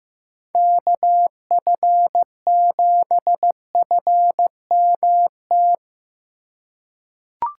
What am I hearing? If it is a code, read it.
Morse code, heard as a keyed tone, about 15 wpm, 700 Hz. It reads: KF7FMT
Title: USA Call Signs - 6 Characters - 4x Character Spacing - 15wpm